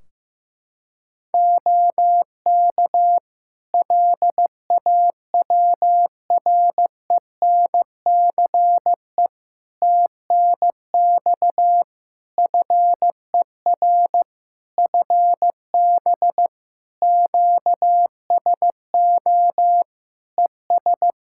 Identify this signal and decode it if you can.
Morse code, heard as a keyed tone, about 15 wpm, 700 Hz. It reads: OK LAWRENCE TNX FER FB QSO ES
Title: Practice QSO - Exchange 4 - Ending - 15wpm